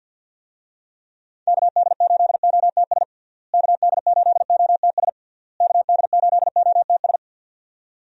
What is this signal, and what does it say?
Morse code, heard as a keyed tone, about 52 wpm, 700 Hz. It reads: KD8OTS KD8OTS KD8OTS
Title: USA Call Signs - 6 Characters - Repeated 3x - 50wpm